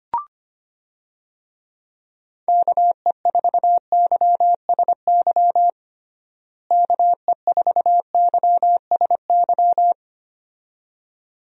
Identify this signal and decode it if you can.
Morse code, heard as a keyed tone, about 25 wpm, 700 Hz. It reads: KE4YSY KE4YSY
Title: USA Call Signs - 6 Characters - Repeated 2x - 25wpm